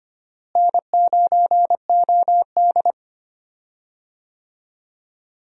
Morse code, heard as a keyed tone, about 25 wpm, 700 Hz. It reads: N9OD
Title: USA Call Signs - 4 Characters - Repeated 2x - 25wpm